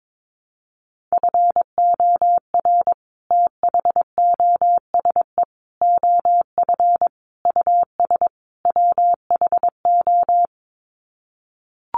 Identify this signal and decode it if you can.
Morse code, heard as a keyed tone, about 22 wpm, 700 Hz. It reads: FOR THOSE OF US WHO
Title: Sets of 5 Words Encore 22wpm